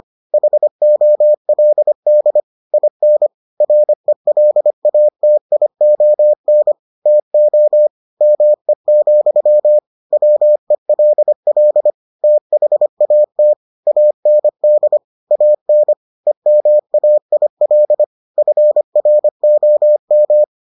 Morse code, heard as a keyed tone, about 25 wpm, 600 Hz. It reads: HOLD IN RELATION TO ME, WELL THAT AND AN EMAIL FROM